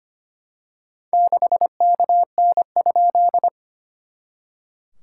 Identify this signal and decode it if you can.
Morse code, heard as a keyed tone, about 25 wpm, 690 Hz. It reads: 6KN?
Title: SOTA QSO Practice 25wpm